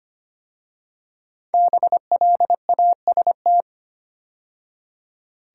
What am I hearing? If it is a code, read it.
Morse code, heard as a keyed tone, about 25 wpm, 700 Hz. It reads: BLAST
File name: 5 Letter Words - First 3 Letters Given - 25wpm